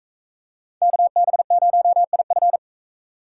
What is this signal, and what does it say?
Morse code, heard as a keyed tone, about 39 wpm, 700 Hz. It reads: KB0IF